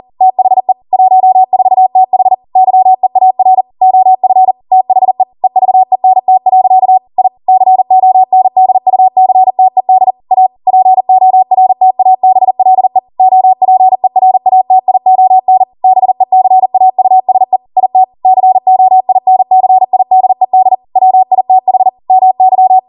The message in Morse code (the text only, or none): THE 14TH YEAR OF THE EVENT. I CONDUCTED A PORTABLE OPERATION BECAUSE IT COINCIDED WITH MY